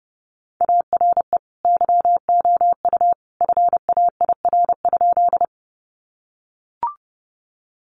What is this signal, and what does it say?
Morse code, heard as a keyed tone, about 30 wpm, 700 Hz. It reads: ARE YOU FAIR?
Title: Sentences from Top 1300 Words 30wpm